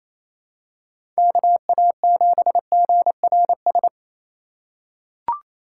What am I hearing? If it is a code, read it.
Morse code, heard as a keyed tone, about 28 wpm, 700 Hz. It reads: KA7GRS
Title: USA Call Signs - 4x Character Spacing - 28wpm